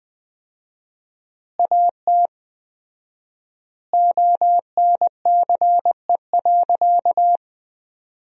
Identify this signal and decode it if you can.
Morse code, heard as a keyed tone, about 20 wpm, 700 Hz. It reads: AT ONCE.
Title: 4x Word-Spacing - Sentences from Top 300 Words - 20wpm